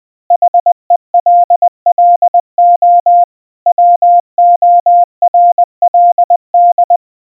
Morse code, heard as a keyed tone, about 20 wpm, 700 Hz. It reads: HELLO WORLD
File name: morse / hello-world